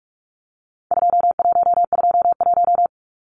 Morse code, heard as a keyed tone, about 44 wpm, 700 Hz. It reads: 2121
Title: Numbers - 4-Digit 45wpm